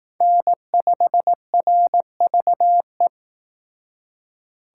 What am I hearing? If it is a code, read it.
Morse code, heard as a keyed tone, about 18 wpm, 700 Hz. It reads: N5RVE
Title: Speed Racing - USA 1x3 Call Signs 15wpm